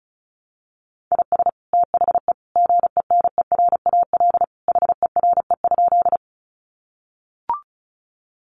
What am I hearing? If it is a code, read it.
Morse code, heard as a keyed tone, about 34 wpm, 700 Hz. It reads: IS THE GENERAL HERE?
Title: Sentences from Top 600 Words 35wpm